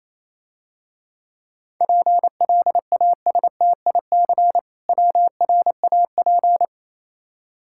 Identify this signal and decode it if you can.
Morse code, heard as a keyed tone, about 28 wpm, 700 Hz. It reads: PLASTIC WRAP